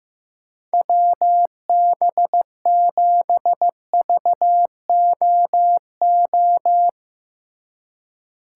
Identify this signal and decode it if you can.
Morse code, heard as a keyed tone, about 15 wpm, 700 Hz. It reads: WB7VOO
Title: USA Call Signs - 2x Character Spacing - 15wpm